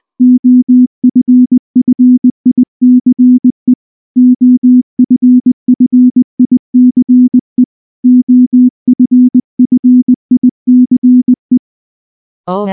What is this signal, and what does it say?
Morse code, heard as a keyed tone, about 19 wpm, 250 Hz. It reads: OFFICE OFFICE OFFICE
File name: morse / morse-training-20WPM-3